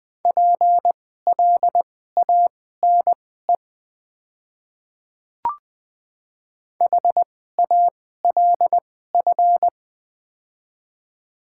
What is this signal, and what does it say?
Morse code, heard as a keyed tone, about 20 wpm, spaced as for 14 wpm, 700 Hz. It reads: PLANE HALF
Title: 2x Character Spacing - Top 500 Words - 20wpm